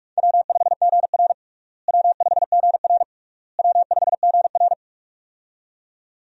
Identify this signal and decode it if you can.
Morse code, heard as a keyed tone, about 46 wpm, 700 Hz. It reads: W5GR W5GR W5GR
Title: USA Call Signs - 4 Characters - Repeated 3x - 45wpm